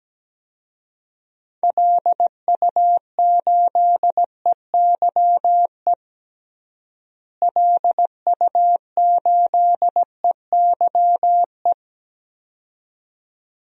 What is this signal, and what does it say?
Morse code, heard as a keyed tone, about 17 wpm, 700 Hz. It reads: LU8EYE LU8EYE
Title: DXCC-100 Call Signs - 6 Characters - Repeated 2x - 17wpm